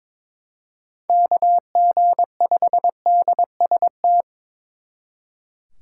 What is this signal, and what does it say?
Morse code, heard as a keyed tone, about 26 wpm, 700 Hz. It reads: KG5DST
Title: Speed Racing - USA 2x3 Call Signs 22wpm